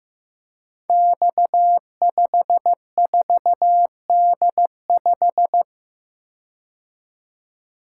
Morse code, heard as a keyed tone, about 15 wpm, 700 Hz. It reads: X54D5
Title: Code Groups - 5 Letter:Numbers - 15wpm